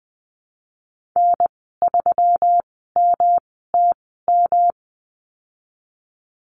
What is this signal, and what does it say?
Morse code, heard as a keyed tone, about 20 wpm, spaced as for 15 wpm, 700 Hz. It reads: N3MTM
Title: 128 Lesson 21g - Number 6 - Call Signs 20wpm